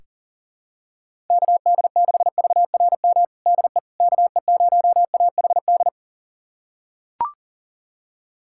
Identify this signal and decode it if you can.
Morse code, heard as a keyed tone, about 37 wpm, 700 Hz. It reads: KD6VRM DE KE0AHD
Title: USA Callsign DE Callsign 40wpm